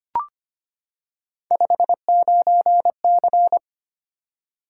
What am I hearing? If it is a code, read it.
Morse code, heard as a keyed tone, about 25 wpm, 700 Hz. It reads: 59C
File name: Three Letter-Number Combinations - 25wpm